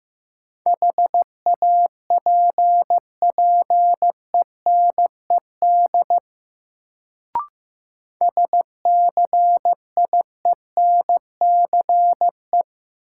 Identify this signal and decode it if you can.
Morse code, heard as a keyed tone, about 15 wpm, 700 Hz. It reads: HAPPENED SCIENCE
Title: Sentences from Top 1000 Words - Review Part 2 - New Word Drill - 15wpm